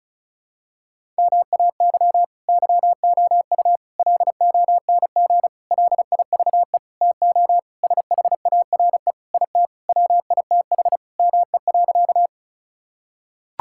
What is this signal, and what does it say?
Morse code, heard as a keyed tone, about 36 wpm, 700 Hz. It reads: MAY YOU LONG LIVE TO SHARE IT WITH ME.
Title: Sentences from Top 1000 Words - Review Part 6 - 35wpm